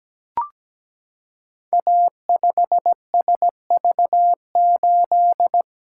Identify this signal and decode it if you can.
Morse code, heard as a keyed tone, about 17 wpm, 700 Hz. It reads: A5SV8
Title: Code Groups - 5 Letter:Numbers - 17wpm